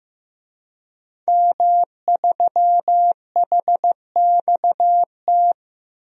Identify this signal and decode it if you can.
Morse code, heard as a keyed tone, about 15 wpm, 700 Hz. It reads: M3HXT
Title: DXCC-100 Call Signs - 5 Characters - 15wpm